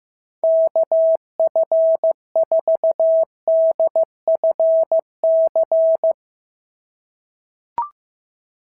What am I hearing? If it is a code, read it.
Morse code, heard as a keyed tone, about 14 wpm, 650 Hz. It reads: KF4DFC
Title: SOTA QSO Practice 15wpm